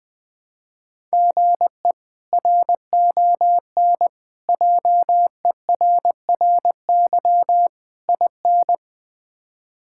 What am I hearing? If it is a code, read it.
Morse code, heard as a keyed tone, about 20 wpm, 700 Hz. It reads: GE RON JERRY IN
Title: K1USN SST QSO Practice - 20wpm